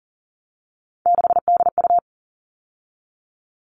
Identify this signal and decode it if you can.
Morse code, heard as a keyed tone, about 40 wpm, 700 Hz. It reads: 6DU
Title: All Cut Numbers - 3-Digit 40wpm